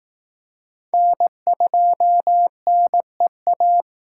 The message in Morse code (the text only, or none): N2NEA